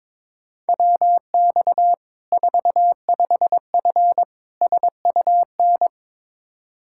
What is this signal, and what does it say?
Morse code, heard as a keyed tone, about 22 wpm, 700 Hz. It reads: WX 45F SUN
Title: K1USN SST QSO Practice 22wpm